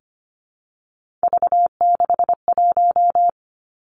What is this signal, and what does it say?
Morse code, heard as a keyed tone, about 25 wpm, 700 Hz. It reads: V61
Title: Three Challenging Letter-Number Combinations 25wpm